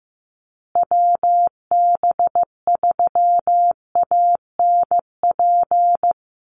Morse code, heard as a keyed tone, about 15 wpm, 700 Hz. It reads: WB3ANP
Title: USA Call Signs Repeated 2x 15wpm